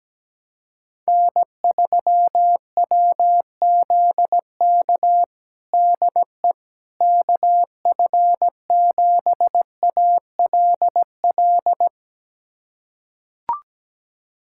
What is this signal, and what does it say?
Morse code, heard as a keyed tone, about 17 wpm, 700 Hz. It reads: N3WZK DE KF7ALL
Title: USA Callsign DE Callsign 17wpm